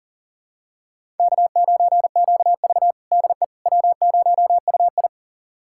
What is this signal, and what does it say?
Morse code, heard as a keyed tone, about 40 wpm, 700 Hz. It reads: K9QV DE W0UI
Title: USA Callsign DE Callsign - 4 Characters - 40wpm